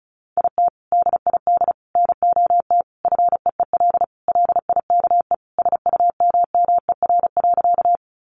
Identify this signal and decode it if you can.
Morse code, heard as a keyed tone, about 34 wpm, 700 Hz. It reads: IT DID NOT FEEL LIKE SUMMER.
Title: Sentences from Top 1100 Words 35wpm